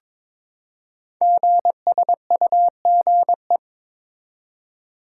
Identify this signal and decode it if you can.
Morse code, heard as a keyed tone, about 22 wpm, 700 Hz. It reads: GSUGE
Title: Code Groups - 5 Letters - 22wpm